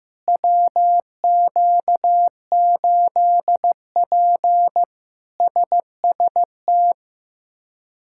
Morse code, heard as a keyed tone, about 15 wpm, 700 Hz. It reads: WQ8P SST
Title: K1USN SST QSO Practice - 15wpm